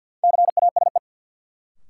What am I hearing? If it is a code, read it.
Morse code, heard as a keyed tone, about 50 wpm, 700 Hz. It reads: CASE
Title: Sets of 2 Words - New Word Drill - 50wpm